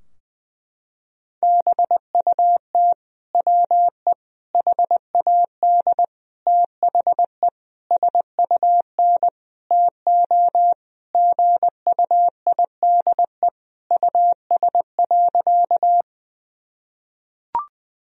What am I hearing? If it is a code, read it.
Morse code, heard as a keyed tone, about 21 wpm, 700 Hz. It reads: BUT WE HAD THE SUN TO GUIDE US.
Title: Sentences from Top 1000 Words - Review Part 3 - 20wpm